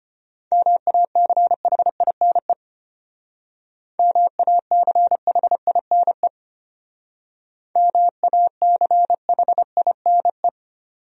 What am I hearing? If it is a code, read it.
Morse code, heard as a keyed tone, about 33 wpm, 700 Hz. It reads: MACHINE MACHINE MACHINE